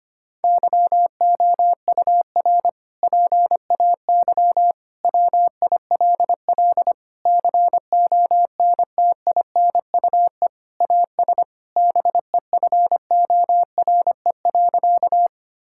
Morse code, heard as a keyed tone, about 25 wpm, 700 Hz. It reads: YOUR PAY WILL CONTINUE AS BEFORE.
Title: Sentences from Top 1000 Words - Review Part 6 - 25wpm